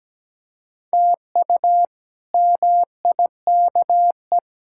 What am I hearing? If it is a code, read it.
Morse code, heard as a keyed tone, about 17 wpm, 700 Hz. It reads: TU MIKE
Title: K1USN SST QSO Practice - 17wpm